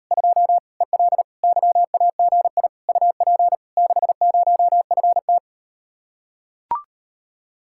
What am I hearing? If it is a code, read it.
Morse code, heard as a keyed tone, about 39 wpm, 700 Hz. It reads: 2 EL YAGI UP 60FT